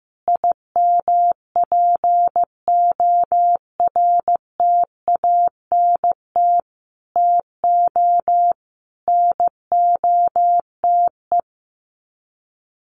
Morse code, heard as a keyed tone, about 15 wpm, 700 Hz. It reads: IMPORTANT TO NOTE
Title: Sets of 5 Words 15wpm